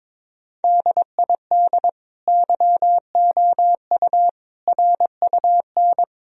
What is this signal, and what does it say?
Morse code, heard as a keyed tone, about 22 wpm, 700 Hz. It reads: DID YOU RUN
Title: Sentences from Top 1000 Words - Review Part 1 - 22wpm